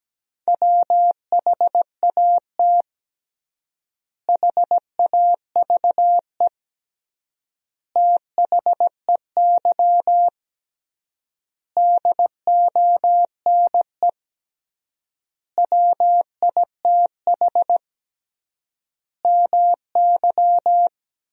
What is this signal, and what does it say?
Morse code, heard as a keyed tone, about 17 wpm, 700 Hz. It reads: WHAT HAVE THEY DONE WITH MY
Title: 3x Word-Spacing - Sentences from Top 500 Words - 17wpm